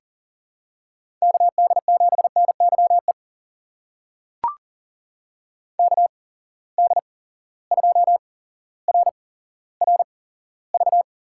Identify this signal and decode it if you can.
Morse code, heard as a keyed tone, about 37 wpm, spaced as for 17 wpm, 700 Hz. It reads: KD7NYE KD2RRV
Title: USA Call Signs - 6 Characters - 8x Character Spacing - 40wpm